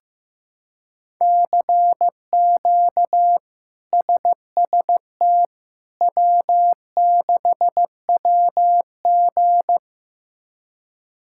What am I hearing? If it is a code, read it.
Morse code, heard as a keyed tone, about 15 wpm, 700 Hz. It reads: CQ SST W6WG
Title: K1USN SST QSO Practice - 15wpm